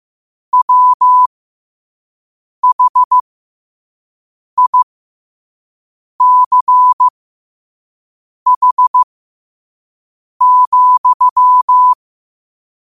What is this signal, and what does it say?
Morse code, heard as a keyed tone, about 15 wpm, spaced as for 5 wpm, 1000 Hz. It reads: WHICH,
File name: qotdfiles / 5wpm